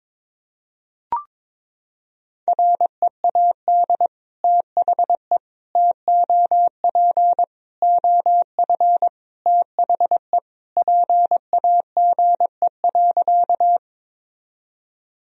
Morse code, heard as a keyed tone, about 22 wpm, 700 Hz. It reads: READ THE TOP OF THE PAGE.
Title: Sentences from Top 1000 Words - Review Part 2 - 22wpm